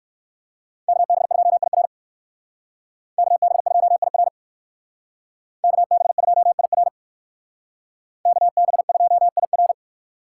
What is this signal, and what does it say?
Morse code, heard as a keyed tone, about 58 wpm, 700 Hz. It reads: KB2IR KB2IR KB2IR KB2IR